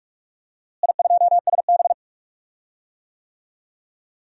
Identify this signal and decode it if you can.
Morse code, heard as a keyed tone, about 46 wpm, 690 Hz. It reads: I2SB
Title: DXCC-100 Call Signs - 4 Characters - Repeated 3x - 45wpm